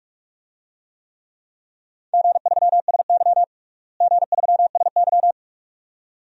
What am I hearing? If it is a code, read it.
Morse code, heard as a keyed tone, about 42 wpm, 700 Hz. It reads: G3SY G3SY
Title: DXCC-100 Call Signs - 4 Characters - Repeated 2x - 45wpm